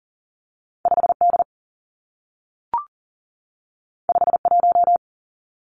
Morse code, heard as a keyed tone, about 41 wpm, 700 Hz. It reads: 5 D 5 1
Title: Two Challenging Letter-Number Combinations 40wpm